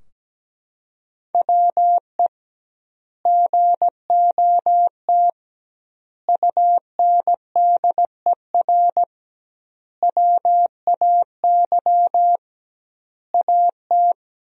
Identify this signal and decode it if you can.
Morse code, heard as a keyed tone, about 17 wpm, 700 Hz. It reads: WE GOT UNDER WAY AT
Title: 2x Word-Spacing - Sentences from Top 300 Words - 17wpm